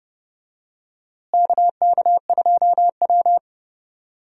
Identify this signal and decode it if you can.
Morse code, heard as a keyed tone, about 32 wpm, 700 Hz. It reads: KK2W
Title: USA Call Signs - 4 Characters - Repeated 2x - 30wpm